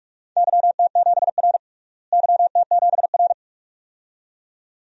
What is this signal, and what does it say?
Morse code, heard as a keyed tone, about 46 wpm, 700 Hz. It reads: YT7R YT7R
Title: DXCC-100 Call Signs - 4 Characters - Repeated 2x - 45wpm